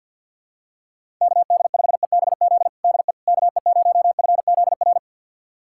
Morse code, heard as a keyed tone, about 50 wpm, 700 Hz. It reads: KD5EBZ DE KE0FZR
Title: USA Callsign DE Callsign - 6 Characters - 50wpm